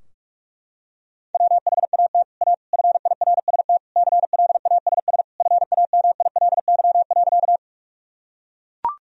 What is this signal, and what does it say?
Morse code, heard as a keyed tone, about 43 wpm, 700 Hz. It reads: WHAT A FIRST CLASS FAMILY.
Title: Sentences from Top 1000 Words - Review Part 2 - 45wpm